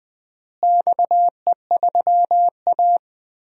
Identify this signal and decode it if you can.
Morse code, heard as a keyed tone, about 21 wpm, 700 Hz. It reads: XE3A
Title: DXCC-100 Call Signs - 4 Characters - Repeated 2x - 20wpm